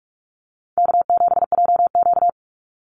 Morse code, heard as a keyed tone, about 44 wpm, 700 Hz. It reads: K7JQ
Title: Easy Call Signs Repeated 3x - 45wpm